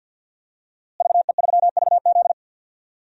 Morse code, heard as a keyed tone, about 52 wpm, 700 Hz. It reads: VE3VZ
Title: DXCC-100 Call Signs - 5 Characters - Repeated 2x - 50wpm